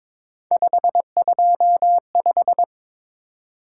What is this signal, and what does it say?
Morse code, heard as a keyed tone, about 22 wpm, 700 Hz. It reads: 525
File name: practice / 22wpm / Three Letter-Number Combinations - 22wpm